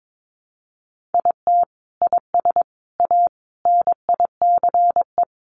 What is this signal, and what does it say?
Morse code, heard as a keyed tone, about 22 wpm, 700 Hz. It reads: IT IS A NICE
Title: Sentences from Top 1100 Words 22wpm